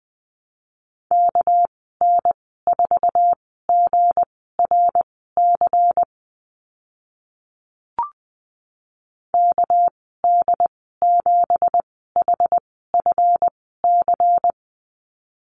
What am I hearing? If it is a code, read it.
Morse code, heard as a keyed tone, about 20 wpm, spaced as for 14 wpm, 700 Hz. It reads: KN4GRC KD7HFC
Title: 213 Lesson 34g - Number 8 - Call Signs 20wpm